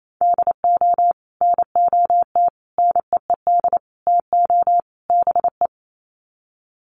Morse code, heard as a keyed tone, about 28 wpm, 700 Hz. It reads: DO NOT NEED TO BE
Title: Sets of 5 Words Encore 28wpm